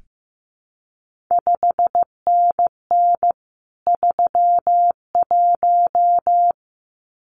Morse code, heard as a keyed tone, about 15 wpm, 700 Hz. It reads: 5NN 31
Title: CQ WW DX Contest Exchange - 15wpm